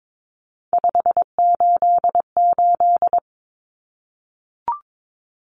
Morse code, heard as a keyed tone, about 22 wpm, 700 Hz. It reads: 588
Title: Numbers - 3-Digit 22wpm